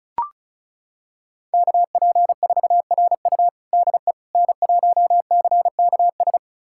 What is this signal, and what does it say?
Morse code, heard as a keyed tone, about 36 wpm, 700 Hz. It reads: KP4RU DE N1CKS
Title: USA Callsign DE Callsign - 5 Characters - 35wpm